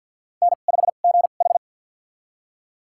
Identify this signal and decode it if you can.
Morse code, heard as a keyed tone, about 51 wpm, spaced as for 36 wpm, 700 Hz. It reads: N5GH
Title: USA Call Signs - 4 Characters - 2x Character Spacing - 50wpm